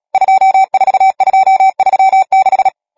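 Morse code, heard as a keyed tone, about 37 wpm, 770 Hz. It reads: 24236
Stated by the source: Morse code number practice, machine-sent training text (Nácvik telegrafie, OK1FCB).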